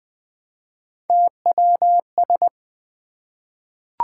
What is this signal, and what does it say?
Morse code, heard as a keyed tone, about 20 wpm, 700 Hz. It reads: TWS
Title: Three Letter-Number Combinations - 20wpm